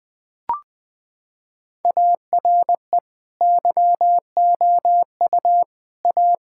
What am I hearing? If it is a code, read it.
Morse code, heard as a keyed tone, about 21 wpm, 700 Hz. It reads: ARE YOU A
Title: Sentences from Top 1000 Words - Review Part 6 - 20wpm